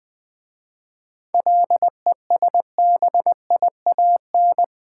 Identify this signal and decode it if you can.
Morse code, heard as a keyed tone, about 20 wpm, 700 Hz. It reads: LESBIAN
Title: 7 Letter Words - First 3 Letters Given - 20wpm